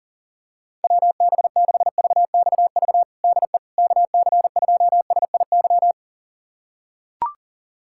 Morse code, heard as a keyed tone, about 40 wpm, 700 Hz. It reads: WB6VXV DE KC2SIY
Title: USA Callsign DE Callsign 40wpm